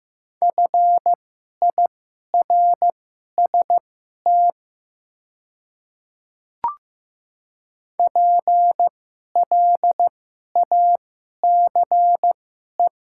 Morse code, heard as a keyed tone, about 15 wpm, spaced as for 11 wpm, 700 Hz. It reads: FIRST PLACE
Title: 2x Character Spacing - Top 200 Words - 15wpm